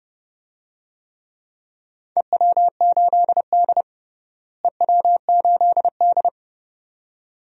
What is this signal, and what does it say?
Morse code, heard as a keyed tone, about 31 wpm, 700 Hz. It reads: EW8D EW8D
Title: DXCC-100 Call Signs - 4 Characters - Repeated 2x - 30wpm